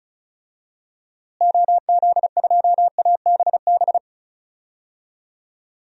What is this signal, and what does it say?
Morse code, heard as a keyed tone, about 36 wpm, 700 Hz. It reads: OZ2ABB
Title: DXCC-100 Call Signs - 6 Characters - Repeated 3x - 35wpm